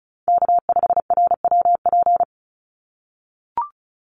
Morse code, heard as a keyed tone, about 34 wpm, 700 Hz. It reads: K5RWP
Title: NA SOTA Chaser Callsigns 35wpm